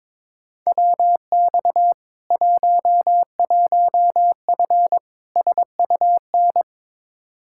Morse code, heard as a keyed tone, about 23 wpm, 700 Hz. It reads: WX 11F SUN
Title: Practice QSO - Exchange 2 - Info - 22wpm